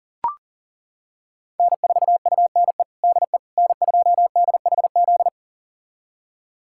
Morse code, heard as a keyed tone, about 37 wpm, 700 Hz. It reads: N4UNE DE N2DHZ
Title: USA Callsign DE Callsign - 5 Characters - 40wpm